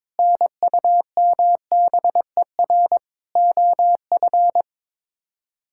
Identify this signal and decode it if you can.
Morse code, heard as a keyed tone, about 22 wpm, 700 Hz. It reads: NUMBER OF
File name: practice / 22wpm / Sets of 8 Words - 22wpm